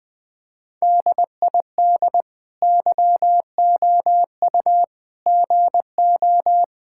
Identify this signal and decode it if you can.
Morse code, heard as a keyed tone, about 20 wpm, 700 Hz. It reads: DID YOU GO
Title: Sentences from Top 1000 Words - Review Part 5 - 20wpm